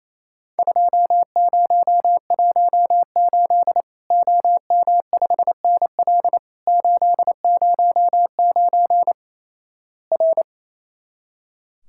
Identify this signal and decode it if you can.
Morse code, heard as a keyed tone, about 28 wpm, 700 Hz. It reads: 2018 OM5NL 809 R
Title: Worked All Europe DX Contest - QTC Exchange - 28wpm